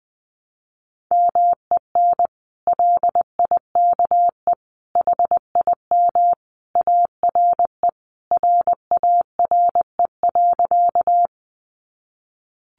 Morse code, heard as a keyed tone, about 20 wpm, 700 Hz. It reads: MEN LIKE HIM ARE RARE.
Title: Sentences from Top 1100 Words 20wpm